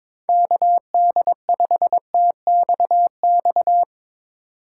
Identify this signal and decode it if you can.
Morse code, heard as a keyed tone, about 22 wpm, 700 Hz. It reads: KD5TXX
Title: K1USN SST QSO Practice - 22wpm